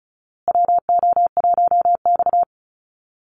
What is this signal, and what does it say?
Morse code, heard as a keyed tone, about 34 wpm, 700 Hz. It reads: WO1X
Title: Call Signs Repeated 2x - 35wpm